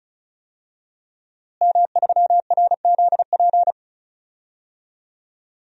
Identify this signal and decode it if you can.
Morse code, heard as a keyed tone, about 36 wpm, 700 Hz. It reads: M3RZP
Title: DXCC-100 Call Signs - 5 Characters - Repeated 2x - 35wpm